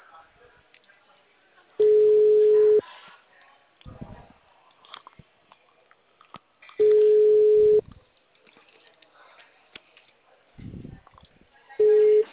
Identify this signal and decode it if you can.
Ringback tone (Europe, 425 Hz), with speech.
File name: out- -2002- - - .16124